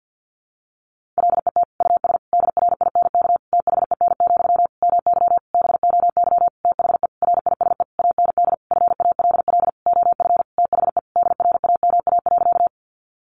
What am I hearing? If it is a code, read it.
Morse code, heard as a keyed tone, about 50 wpm, 700 Hz. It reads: LET US DRINK THEN, MY BOY THE RISE AND FALL OF THE DRAMA.
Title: Sentences from Top 1500 Words 50wpm